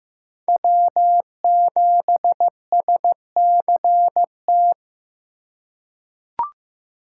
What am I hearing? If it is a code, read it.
Morse code, heard as a keyed tone, about 15 wpm, 700 Hz. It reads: W7SCT
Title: USA Call Signs - 3x Character Spacing - 15wpm